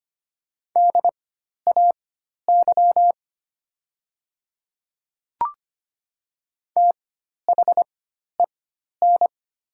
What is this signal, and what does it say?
Morse code, heard as a keyed tone, about 25 wpm, spaced as for 12 wpm, 700 Hz. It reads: DAY THEN
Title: 4x Character Spacing - Top 100 Words - 25wpm